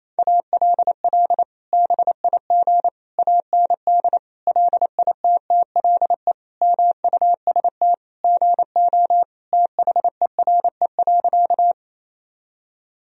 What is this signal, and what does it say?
Morse code, heard as a keyed tone, about 27 wpm, 700 Hz. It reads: ALL BIG AND LITTLE MUST GO THERE.
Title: Sentences from Top 1000 Words - Review Part 1 - 28wpm